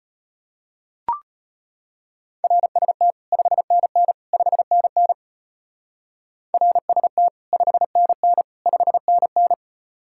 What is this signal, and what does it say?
Morse code, heard as a keyed tone, about 36 wpm, 700 Hz. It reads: RST 5NN 5NN RST 5NN 5NN